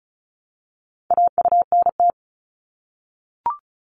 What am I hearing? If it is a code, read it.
Morse code, heard as a keyed tone, about 36 wpm, 700 Hz. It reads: AUNT
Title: Multi-letter Phonemes Repeated 3x 35wpm